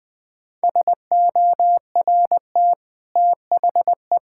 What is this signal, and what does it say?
Morse code, heard as a keyed tone, about 20 wpm, 700 Hz. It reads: SORT THE
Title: Easy Sentences with Hints - 20wpm